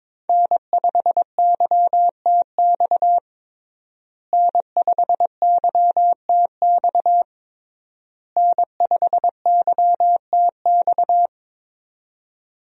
Morse code, heard as a keyed tone, about 22 wpm, 700 Hz. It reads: N5YTX N5YTX N5YTX
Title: USA Call Signs - 5 Characters - Repeated 3x - 22wpm